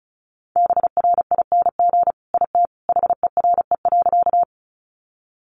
Morse code, heard as a keyed tone, about 34 wpm, 700 Hz. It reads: BRING IT HERE.
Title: Sentences from Top 500 Words 35wpm